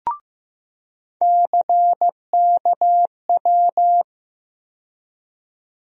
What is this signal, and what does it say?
Morse code, heard as a keyed tone, about 15 wpm, 700 Hz. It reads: CKW
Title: USA Call Sign Postfixes - 3 Letters 15wpm